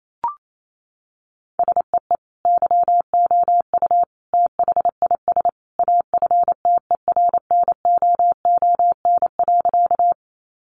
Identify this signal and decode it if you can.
Morse code, heard as a keyed tone, about 28 wpm, 700 Hz. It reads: SEE YOU THIS AFTERNOON.
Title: Sentences from Top 1500 Words 28wpm